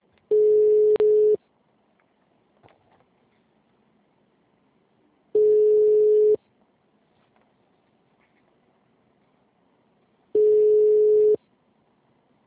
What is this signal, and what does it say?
Ringback tone (Europe, 425 Hz).